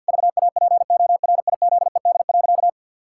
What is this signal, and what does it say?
Morse code, heard as a keyed tone, about 52 wpm, 700 Hz. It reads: VAPORIZED.
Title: Pangram Challenge v2 - 50wpm